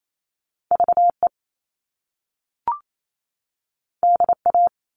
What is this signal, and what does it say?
Morse code, heard as a keyed tone, about 28 wpm, 700 Hz. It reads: VE DA
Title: All Cut Numbers - 2-Digit 28wpm